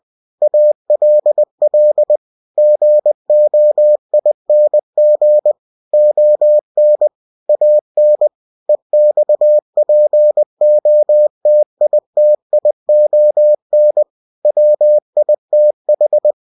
Morse code, heard as a keyed tone, about 20 wpm, 600 Hz. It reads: ALL GOING ON AN EXPOTITION WITH